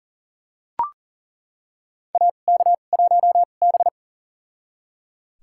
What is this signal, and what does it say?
Morse code, heard as a keyed tone, about 37 wpm, spaced as for 28 wpm, 700 Hz. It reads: AK1B
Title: USA Call Signs - 4 Characters - 2x Character Spacing - 40wpm